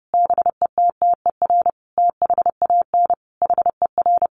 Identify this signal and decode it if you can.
Morse code, heard as a keyed tone, about 30 wpm, 700 Hz. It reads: BETTER THAN HER
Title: Sentences from Top 400 Words 30wpm